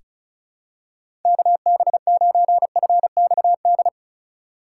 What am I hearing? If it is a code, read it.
Morse code, heard as a keyed tone, about 34 wpm, 700 Hz. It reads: KB9FXD
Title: USA Call Signs - 6 Characters - 2x Character Spacing - 35wpm